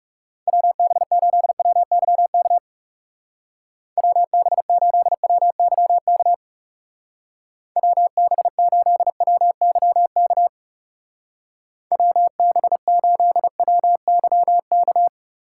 Morse code, heard as a keyed tone, about 36 wpm, 700 Hz. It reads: WB8WYK WB8WYK WB8WYK WB8WYK
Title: Speed Racing - USA 2x3 Call Signs 30wpm